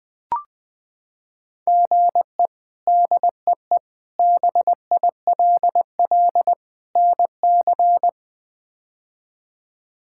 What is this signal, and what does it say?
Morse code, heard as a keyed tone, about 20 wpm, 700 Hz. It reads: GE DEE BILL NC
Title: K1USN SST QSO Practice - 20wpm